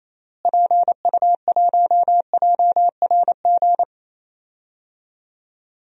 Morse code, heard as a keyed tone, about 28 wpm, 700 Hz. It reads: PU1JRG
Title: DXCC-100 Call Signs - 6 Characters - Repeated 2x - 28wpm